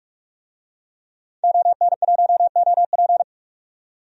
Morse code, heard as a keyed tone, about 46 wpm, 700 Hz. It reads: ON1OP
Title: DXCC-100 Call Signs - 5 Characters - 45wpm